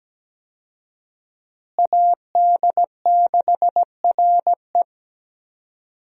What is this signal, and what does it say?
Morse code, heard as a keyed tone, about 17 wpm, 700 Hz. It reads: AD6RE
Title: USA Call Signs - 5 Characters - Repeated 3x - 17wpm